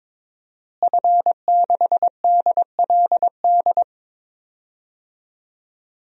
Morse code, heard as a keyed tone, about 21 wpm, 700 Hz. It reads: F6DLD
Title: DXCC-100 Call Signs - 5 Characters - Repeated 3x - 22wpm